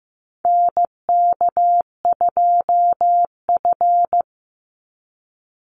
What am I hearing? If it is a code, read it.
Morse code, heard as a keyed tone, about 15 wpm, 700 Hz. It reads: NK2F
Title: Call Signs Repeated 3x - 15wpm